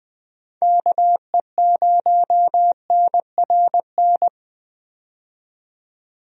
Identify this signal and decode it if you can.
Morse code, heard as a keyed tone, about 20 wpm, 700 Hz. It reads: KE0NRN
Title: USA Call Signs - 6 Characters - 20wpm